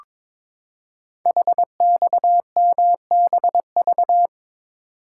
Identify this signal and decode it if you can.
Morse code, heard as a keyed tone, about 22 wpm, 700 Hz. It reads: HXMBV
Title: Code Groups - 5 Letters - 22wpm